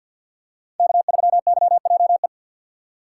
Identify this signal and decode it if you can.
Morse code, heard as a keyed tone, about 50 wpm, 700 Hz. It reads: K3YJE
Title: USA Call Signs - 5 Characters - 6x Character Spacing - 50wpm